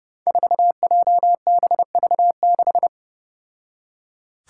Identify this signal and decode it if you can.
Morse code, heard as a keyed tone, about 32 wpm, 700 Hz. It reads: 4JBV6
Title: Five Challenging Letter-Number Combinations 30wpm